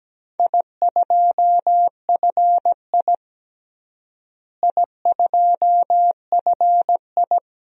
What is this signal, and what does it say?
Morse code, heard as a keyed tone, about 17 wpm, 700 Hz. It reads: I2FI I2FI
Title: DXCC-100 Call Signs - 4 Characters - Repeated 2x - 17wpm